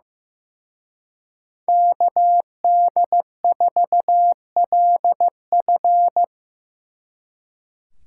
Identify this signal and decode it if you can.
Morse code, heard as a keyed tone, about 18 wpm, 700 Hz. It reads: KD4LF
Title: Speed Racing - USA 2x2 Call Signs 15wpm